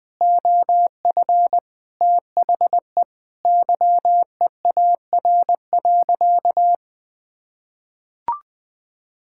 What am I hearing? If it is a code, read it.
Morse code, heard as a keyed tone, about 20 wpm, 700 Hz. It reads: OF THE YEAR.
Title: Sentences with 8-Word Phrases - 20wpm